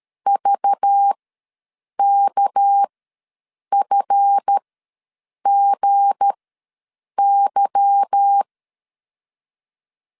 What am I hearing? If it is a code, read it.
Morse code, heard as a keyed tone, about 13 wpm, spaced as for 7 wpm, 800 Hz. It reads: VKFGY
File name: LEZIONI / 8 YJ45 / 8 Y ws 12wpm 3409